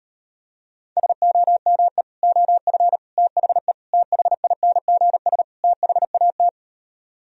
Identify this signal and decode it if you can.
Morse code, heard as a keyed tone, about 33 wpm, 700 Hz. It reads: SOME OF THE THINGS THAT